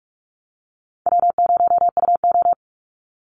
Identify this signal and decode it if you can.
Morse code, heard as a keyed tone, about 44 wpm, 700 Hz. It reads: W0UO
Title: Call Signs Repeated 2x - 45wpm